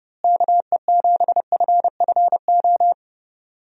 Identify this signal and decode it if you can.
Morse code, heard as a keyed tone, about 30 wpm, 700 Hz. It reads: KE7FFO
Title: USA Call Signs - 6 Characters - Repeated 3x - 30wpm